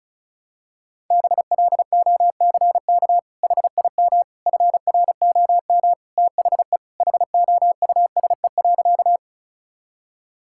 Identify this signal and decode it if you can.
Morse code, heard as a keyed tone, about 36 wpm, 700 Hz. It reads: BLOCK HIM FROM THE HOUSE.
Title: Sentences from Top 1600 Words 35wpm